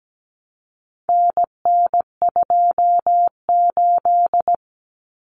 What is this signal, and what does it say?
Morse code, heard as a keyed tone, about 17 wpm, 700 Hz. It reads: NN28
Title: Common Cut Numbers - 4-Digit 17wpm